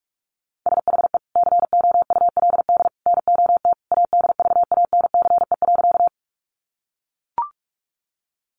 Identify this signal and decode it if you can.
Morse code, heard as a keyed tone, about 44 wpm, 700 Hz. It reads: SHE COULD NOT ADVANCE.
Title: Sentences from Top 1400 Words 45wpm